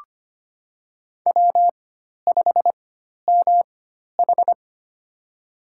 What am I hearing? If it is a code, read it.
Morse code, heard as a keyed tone, about 25 wpm, spaced as for 12 wpm, 700 Hz. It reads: W5MH
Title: USA Call Signs - 4 Characters - 4x Character Spacing - 25wpm